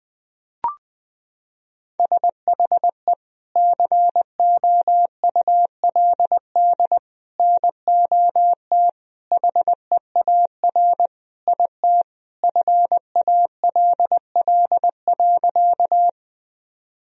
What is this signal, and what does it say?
Morse code, heard as a keyed tone, about 21 wpm, 700 Hz. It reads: SHE COULD NOT HEAR IT FALL.
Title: Sentences from Top 1000 Words - Review Part 2 - 20wpm